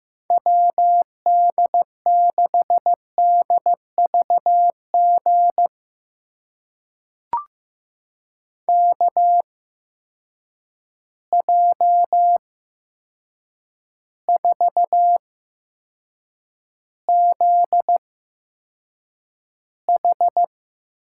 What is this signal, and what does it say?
Morse code, heard as a keyed tone, about 15 wpm, spaced as for 7 wpm, 700 Hz. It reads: WD6DVG KJ4ZH
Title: USA Call Signs - 6 Characters - 8x Character Spacing - 15wpm